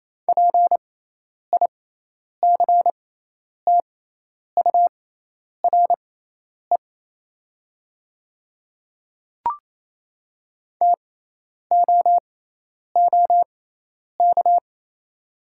Morse code, heard as a keyed tone, about 28 wpm, spaced as for 10 wpm, 700 Hz. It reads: PICTURE TOOK
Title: 6x Character Spacing - Top 300 Words - 28wpm